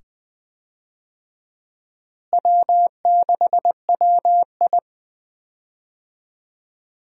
Morse code, heard as a keyed tone, about 20 wpm, 700 Hz. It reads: W6WI
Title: USA Call Signs - 4 Characters - Repeated 2x - 20wpm